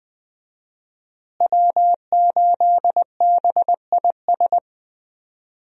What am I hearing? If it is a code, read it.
Morse code, heard as a keyed tone, about 21 wpm, 700 Hz. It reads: W8BIS
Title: USA Call Signs - 5 Characters - Repeated 3x - 20wpm